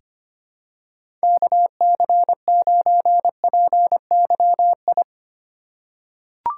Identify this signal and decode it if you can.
Morse code, heard as a keyed tone, about 25 wpm, 700 Hz. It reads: KC9PYI
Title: USA Call Signs - 6 Characters - 3x Character Spacing - 25wpm